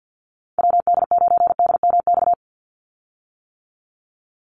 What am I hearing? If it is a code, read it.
Morse code, heard as a keyed tone, about 50 wpm, 700 Hz. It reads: WD9DMX
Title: USA Call Signs 50wpm